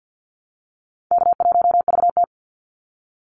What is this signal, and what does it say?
Morse code, heard as a keyed tone, about 50 wpm, 700 Hz. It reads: K1VT
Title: Call Signs - 50wpm